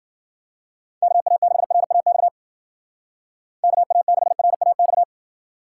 Morse code, heard as a keyed tone, about 52 wpm, 700 Hz. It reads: KA6RAX KA6RAX